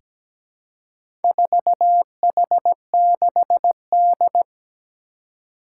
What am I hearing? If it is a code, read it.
Morse code, heard as a keyed tone, about 17 wpm, 700 Hz. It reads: 4H6D
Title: Four Challenging Letter-Number Combinations 17wpm